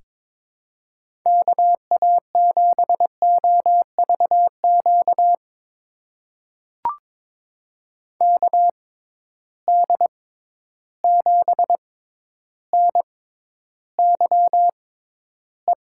Morse code, heard as a keyed tone, about 23 wpm, spaced as for 11 wpm, 700 Hz. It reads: KA7OVQ KD7NYE
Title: USA Call Signs - 6 Characters - 6x Character Spacing - 22wpm